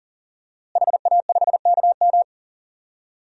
Morse code, heard as a keyed tone, about 40 wpm, 690 Hz. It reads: HA5KM
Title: DXCC-100 Call Signs - 5 Characters - Repeated 2x - 40wpm